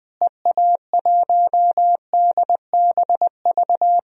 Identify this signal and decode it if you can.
Morse code, heard as a keyed tone, about 20 wpm, 700 Hz. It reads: EA1DBV
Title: DXCC-100 Call Signs - 6 Characters - Repeated 3x - 20wpm